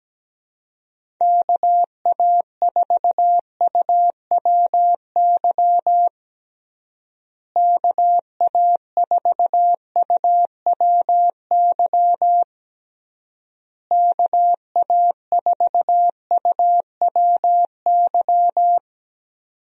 Morse code, heard as a keyed tone, about 17 wpm, 700 Hz. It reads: KA4UWY KA4UWY KA4UWY
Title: USA Call Signs - 6 Characters - Repeated 3x - 17wpm